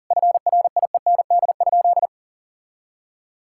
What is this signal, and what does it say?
Morse code, heard as a keyed tone, about 40 wpm, 700 Hz. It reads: FRIEND?
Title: Sentences from Top 1000 Words - Review Part 5 - 40wpm